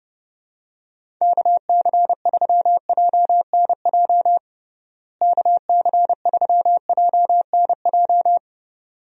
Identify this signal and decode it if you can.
Morse code, heard as a keyed tone, about 32 wpm, 700 Hz. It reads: KC3JNJ KC3JNJ
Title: USA Call Signs - 6 Characters - Repeated 2x - 30wpm